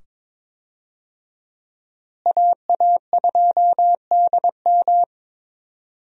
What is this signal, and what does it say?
Morse code, heard as a keyed tone, about 22 wpm, 700 Hz. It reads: AA2DM
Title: USA Call Signs - 5 Characters - Repeated 2x - 22wpm